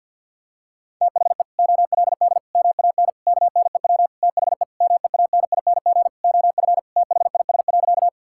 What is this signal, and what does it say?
Morse code, heard as a keyed tone, about 48 wpm, 700 Hz. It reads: THE OLD MAN KNEW THE MEANING OF THIS.
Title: Sentences from Top 1000 Words - Review Part 6 - 50wpm